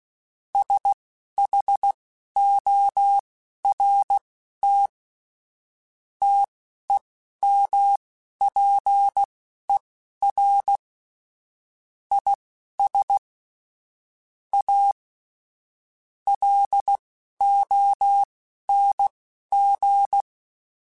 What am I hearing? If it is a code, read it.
Morse code, heard as a keyed tone, about 16 wpm, spaced as for 12 wpm, 800 Hz. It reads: SHORT TEMPER IS A LONG
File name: Quotes / Quotes-mp3s / Temper @10